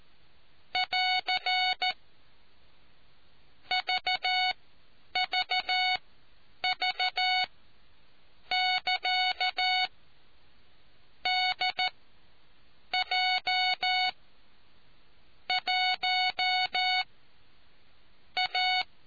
Morse code, heard as a keyed tone, about 14 wpm, spaced as for 6 wpm, 740 Hz. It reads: <AR> VVV <KA> DJ1A